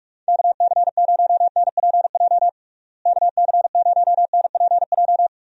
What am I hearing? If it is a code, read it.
Morse code, heard as a keyed tone, about 46 wpm, 700 Hz. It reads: KC0NPJ KC0NPJ
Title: USA Call Signs - 6 Characters - Repeated 3x - 45wpm